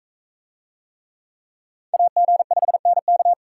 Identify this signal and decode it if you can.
Morse code, heard as a keyed tone, about 42 wpm, 700 Hz. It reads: AG5NK